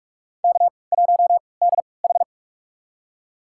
Morse code, heard as a keyed tone, about 46 wpm, spaced as for 26 wpm, 700 Hz. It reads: K1DH
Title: USA Call Signs - 3x Character Spacing - 45wpm